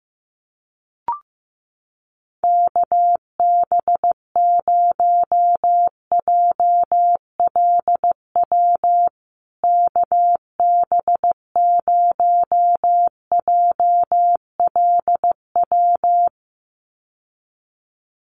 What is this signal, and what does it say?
Morse code, heard as a keyed tone, about 15 wpm, 700 Hz. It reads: KB0JLW KB0JLW
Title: USA Call Signs Repeated 2x 15wpm